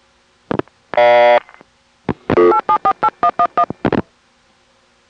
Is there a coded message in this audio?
Yes: DTMF from 2.5 to 3.6 s, heard as telephone touch tones, tones of about 70 ms. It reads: 8748111